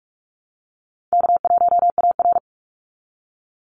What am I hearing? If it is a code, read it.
Morse code, heard as a keyed tone, about 44 wpm, 700 Hz. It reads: K1AR
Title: Call Signs - 45wpm